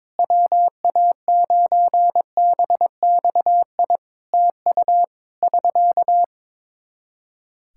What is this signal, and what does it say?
Morse code, heard as a keyed tone, about 22 wpm, 700 Hz. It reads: WA9BXI TU <SK>
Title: Practice QSO - Exchange 4 - Ending - 22wpm